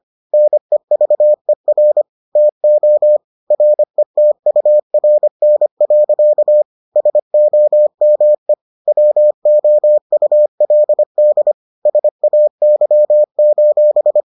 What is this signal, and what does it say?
Morse code, heard as a keyed tone, about 25 wpm, 600 Hz. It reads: NEVER TO RETURN. SOME WOULD SAY: